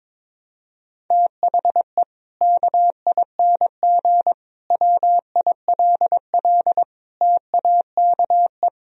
Morse code, heard as a keyed tone, about 22 wpm, 700 Hz. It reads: THE KING WILL TAKE
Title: Sentences from Top 1000 Words - Review Part 4 - 22wpm